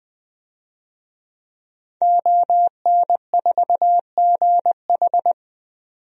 Morse code, heard as a keyed tone, about 21 wpm, 700 Hz. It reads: ON4GH
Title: DXCC-100 Call Signs - 5 Characters - 20wpm